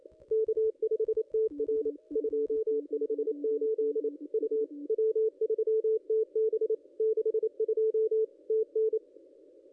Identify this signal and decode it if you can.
Morse code, heard as a keyed tone, about 28 wpm, 440 Hz. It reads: K5TR 258 U W3TB 62 TN
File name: contests / ss-cw-k5tr-op-n5rz -1800